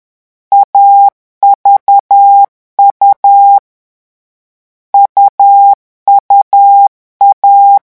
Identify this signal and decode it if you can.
Morse code, heard as a keyed tone, about 10 wpm, 800 Hz. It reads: AVU UUA
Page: Leçon 3 - vitesse 10 mots minute